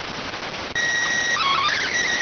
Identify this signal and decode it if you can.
SSTV (slow-scan television), PD 290.